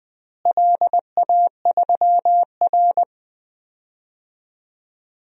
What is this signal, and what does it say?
Morse code, heard as a keyed tone, about 20 wpm, 700 Hz. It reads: LA3R
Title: DXCC-100 Call Signs - 4 Characters - Repeated 3x - 20wpm